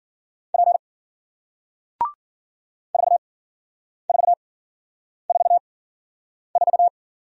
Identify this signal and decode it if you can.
Morse code, heard as a keyed tone, about 55 wpm, spaced as for 9 wpm, 700 Hz. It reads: P 4444